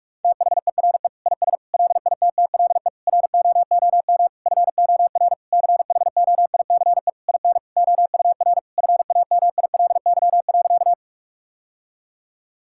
Morse code, heard as a keyed tone, about 46 wpm, 700 Hz. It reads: THERE IS LITTLE ROOM FOR CHOICE IN OUR FAMILY.
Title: Sentences from Top 1000 Words - Review Part 6 - 45wpm